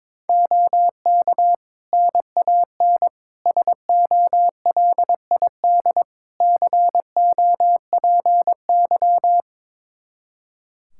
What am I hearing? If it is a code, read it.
Morse code, heard as a keyed tone, about 22 wpm, 700 Hz. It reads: OK NAN SOLID COPY
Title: Practice QSO - Exchange 3 - More Info - 22wpm